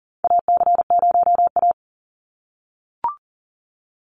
Morse code, heard as a keyed tone, about 39 wpm, 700 Hz. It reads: AC0A
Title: NA SOTA Chaser Callsigns 40wpm